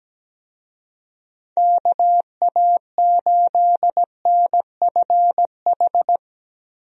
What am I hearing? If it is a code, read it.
Morse code, heard as a keyed tone, about 17 wpm, 700 Hz. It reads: KA8NFH
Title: USA Call Signs - 6 Characters - 17wpm